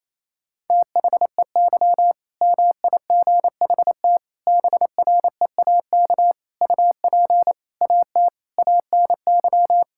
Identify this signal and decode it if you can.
Morse code, heard as a keyed tone, about 27 wpm, 700 Hz. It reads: THEY MIGHT BREAK UP AT ANY
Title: Sentences from Top 1000 Words - Review Part 6 - 28wpm